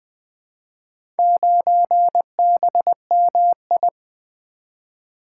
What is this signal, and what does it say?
Morse code, heard as a keyed tone, about 20 wpm, 700 Hz. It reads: 9BMI
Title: Four Letter-Number Combinations - 20wpm